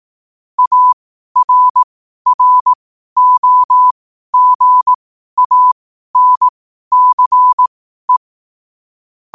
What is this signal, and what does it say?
Morse code, heard as a keyed tone, about 18 wpm, spaced as for 12 wpm, 1000 Hz. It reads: ARROGANCE
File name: qotdfiles / 12wpm